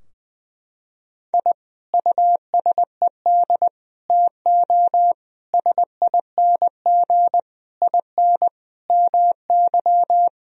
Morse code, heard as a keyed tone, about 20 wpm, 700 Hz. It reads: I USED TO SING IN MY
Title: Sentences from Top 1000 Words - Review Part 2 - 20wpm